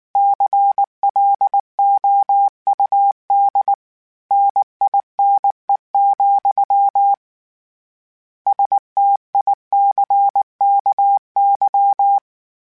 Morse code, heard as a keyed tone, about 19 wpm, 800 Hz. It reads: CLOUD NINE, STICKY
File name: Misc / Misc-mp3s / Word Pairs @18